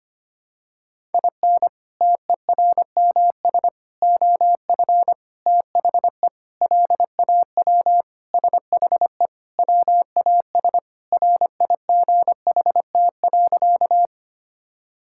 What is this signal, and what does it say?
Morse code, heard as a keyed tone, about 25 wpm, 700 Hz. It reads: IN TERMS OF THE LAW SHE WAS RIGHT.
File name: practice / 25wpm / Sentences with 4-Word Phrases - 25wpm